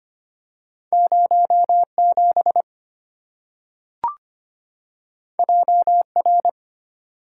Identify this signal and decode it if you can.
Morse code, heard as a keyed tone, about 25 wpm, 690 Hz. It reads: 07 JR
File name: Two Letter-Number Combinations - 25wpm